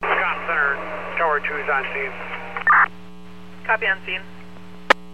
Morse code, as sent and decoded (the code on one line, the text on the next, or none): none